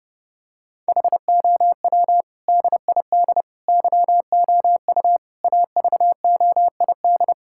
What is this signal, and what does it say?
Morse code, heard as a keyed tone, about 30 wpm, 700 Hz. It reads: HOW DID YOU AVOID
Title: Sentences from Top 1000 Words - Review Part 6 - 30wpm